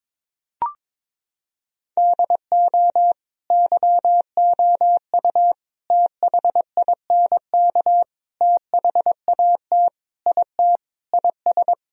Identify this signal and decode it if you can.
Morse code, heard as a keyed tone, about 22 wpm, 700 Hz. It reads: DO YOU THINK THAT IT IS
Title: Sentences from Top 1000 Words - Review Part 4 - 22wpm